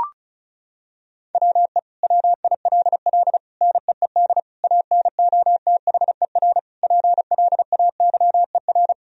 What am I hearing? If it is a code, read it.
Morse code, heard as a keyed tone, about 34 wpm, 700 Hz. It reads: WE WILL NEED ANOTHER PLAYER
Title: Sentences from Top 1000 Words - Review Part 4 - 35wpm